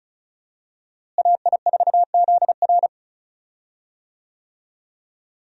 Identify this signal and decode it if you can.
Morse code, heard as a keyed tone, about 34 wpm, 700 Hz. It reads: AI4ZR
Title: USA Call Signs - 5 Characters - Repeated 3x - 35wpm